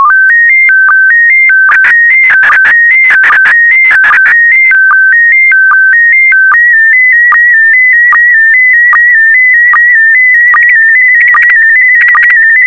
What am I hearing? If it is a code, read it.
SSTV (slow-scan television), PD 160.
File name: sstv / PD160-VK4MSL